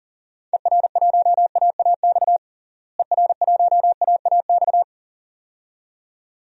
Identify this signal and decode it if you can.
Morse code, heard as a keyed tone, about 40 wpm, 700 Hz. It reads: ER1AAX ER1AAX
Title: DXCC-100 Call Signs - 6 Characters - Repeated 2x - 40wpm